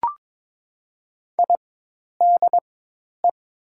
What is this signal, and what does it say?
Morse code, heard as a keyed tone, about 22 wpm, spaced as for 10 wpm, 700 Hz. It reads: IDE
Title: 4x Character Spacing - Top 300 Words - 22wpm